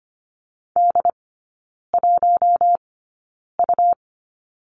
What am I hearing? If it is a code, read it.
Morse code, heard as a keyed tone, about 25 wpm, spaced as for 9 wpm, 700 Hz. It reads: D1U
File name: Lesson 17e - Letter u - Three Characters 25wpm